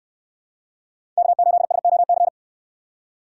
Morse code, heard as a keyed tone, about 69 wpm, 700 Hz. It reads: K8SOC